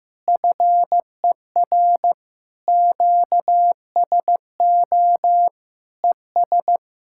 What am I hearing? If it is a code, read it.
Morse code, heard as a keyed tone, about 15 wpm, 700 Hz. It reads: FER QSO ES
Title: Speed-Racing Common QSO Phrases Encore - 15wpm